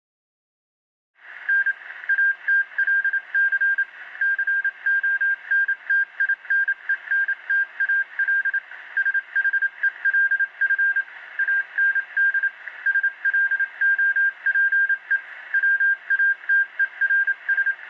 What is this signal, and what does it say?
Morse code, heard as a keyed tone, about 27 wpm, 1600 Hz. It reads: N ATLC CONTINENTAL SHELF AND SLOPE WATERS